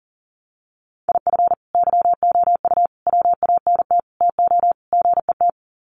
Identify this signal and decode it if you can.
Morse code, heard as a keyed tone, about 40 wpm, 700 Hz. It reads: IF YOU WANT TO GET
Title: Sets of 5 Words Encore 40wpm